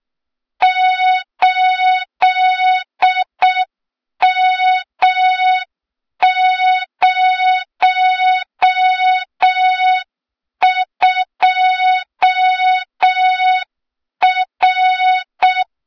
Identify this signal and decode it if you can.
Morse code, heard as a keyed tone, about 6 wpm, 3700 Hz, with noise. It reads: 8M02R
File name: Morze-HA2MN / hang / 23 GYAKORLAT